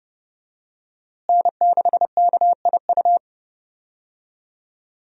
Morse code, heard as a keyed tone, about 32 wpm, 700 Hz. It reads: N6KIU
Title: USA Call Signs - 5 Characters - Repeated 3x - 30wpm